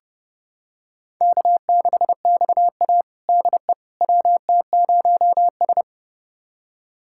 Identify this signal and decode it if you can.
Morse code, heard as a keyed tone, about 30 wpm, 700 Hz. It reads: K6XA DE WT0S
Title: USA Callsign DE Callsign - 4 Characters - 30wpm